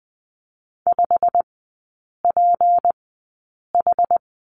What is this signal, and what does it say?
Morse code, heard as a keyed tone, about 20 wpm, spaced as for 8 wpm, 700 Hz. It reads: 5PH
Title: 152 Lesson 25e - Letter p - Three Characters 20wpm